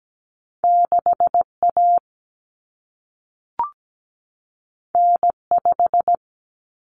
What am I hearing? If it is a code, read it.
Morse code, heard as a keyed tone, about 17 wpm, 700 Hz. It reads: 6A N5
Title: Common Cut Numbers - 2-Digit 17wpm